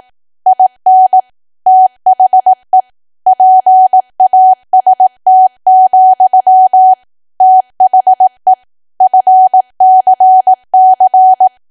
Morse code, heard as a keyed tone, about 18 wpm, 750 Hz. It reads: IN THE PAST, THE FCC
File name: cwa / W1AW-Level2 / 18WPM